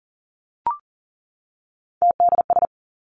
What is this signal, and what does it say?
Morse code, heard as a keyed tone, about 42 wpm, 700 Hz. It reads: TDS
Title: Three Challenging Letter-Number Combinations 40wpm